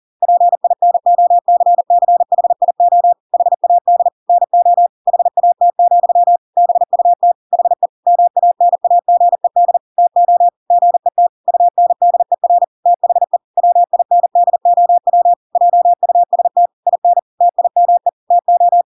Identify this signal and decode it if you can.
Morse code, heard as a keyed tone, about 40 wpm, 700 Hz. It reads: PINOCCHIO HAD NO HAT, BUT HE MANAGED TO GET UNDER THE WINDOW JUST IN TIME TO
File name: Pinocchio / Pinocchio ch6 40 40 ik1hge